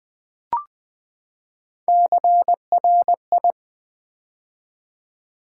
Morse code, heard as a keyed tone, about 20 wpm, 700 Hz. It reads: CRI
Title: Build A Word - 3 Letters At A Time - 6-Letter Words - 20wpm